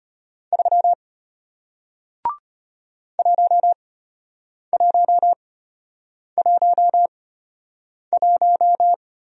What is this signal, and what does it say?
Morse code, heard as a keyed tone, about 33 wpm, spaced as for 8 wpm, 690 Hz. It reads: 3 1111